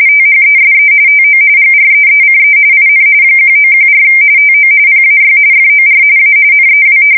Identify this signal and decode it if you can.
RTTY, about 45.45 baud. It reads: THESE TO GREECE IN 1914 TO PAY FOR A NEW
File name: fsk170 / battleships